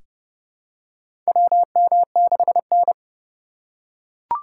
Morse code, heard as a keyed tone, about 30 wpm, 700 Hz. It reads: WM6N
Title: USA Call Signs - 4 Characters - 3x Character Spacing - 30wpm